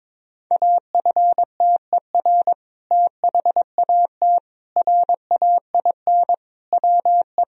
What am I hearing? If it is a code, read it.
Morse code, heard as a keyed tone, about 22 wpm, 700 Hz. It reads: AFTER THAT RAIN WE
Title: Sentences from Top 1000 Words - Review Part 1 - 22wpm